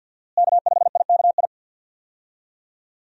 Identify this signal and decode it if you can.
Morse code, heard as a keyed tone, about 46 wpm, 690 Hz. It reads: K5IKI
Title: USA Call Signs - 5 Characters - Repeated 3x - 50wpm